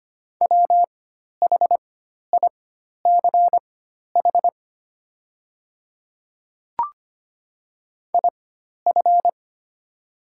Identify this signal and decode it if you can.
Morse code, heard as a keyed tone, about 25 wpm, spaced as for 12 wpm, 700 Hz. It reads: WHICH IF
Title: 4x Character Spacing - Top 100 Words - 25wpm